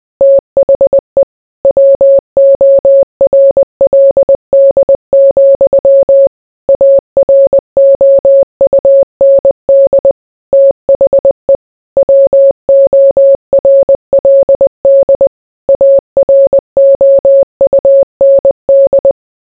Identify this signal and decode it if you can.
Morse code, heard as a keyed tone, about 20 wpm, 550 Hz. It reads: THE WORLD, AROUND THE WORLD AROUND